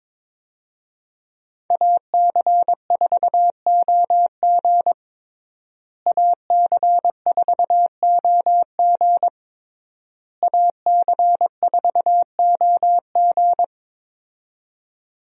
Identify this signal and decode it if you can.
Morse code, heard as a keyed tone, about 21 wpm, 700 Hz. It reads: AC4OG AC4OG AC4OG
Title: USA Call Signs - 5 Characters - Repeated 3x - 22wpm